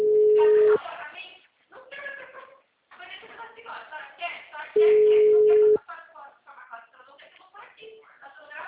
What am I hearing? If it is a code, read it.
Ringback tone (Europe, 425 Hz), with speech.